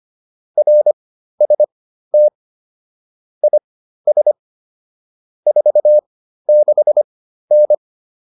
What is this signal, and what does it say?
Morse code, heard as a keyed tone, about 25 wpm, spaced as for 13 wpm, 600 Hz. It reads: RST IS 46N